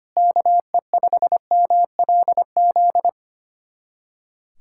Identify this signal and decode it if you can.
Morse code, heard as a keyed tone, about 25 wpm, 700 Hz. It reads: KE5MLZ
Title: USA Callsign DE Callsign - 6 Characters - 25wpm